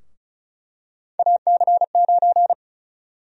Morse code, heard as a keyed tone, about 36 wpm, 700 Hz. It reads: AC9
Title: USA Call Sign Prefixes - Repeated 2x - 35wpm